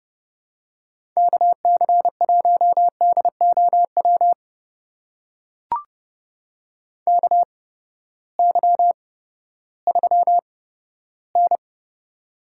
Morse code, heard as a keyed tone, about 30 wpm, spaced as for 15 wpm, 700 Hz. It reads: KC1DOW KY3N
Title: USA Call Signs - 8x Character Spacing - 30wpm